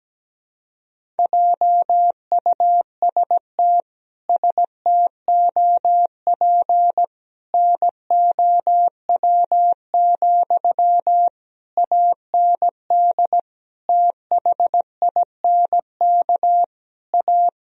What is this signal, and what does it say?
Morse code, heard as a keyed tone, about 17 wpm, 700 Hz. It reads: JUST STOP NOW, AND THINK A
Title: Sentences from Top 1000 Words - Review Part 1 - 17wpm